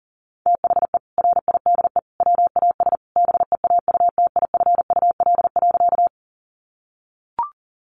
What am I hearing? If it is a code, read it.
Morse code, heard as a keyed tone, about 37 wpm, 700 Hz. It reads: THE RIDE WAS BEAUTIFUL.
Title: Sentences from Top 1200 Words 40wpm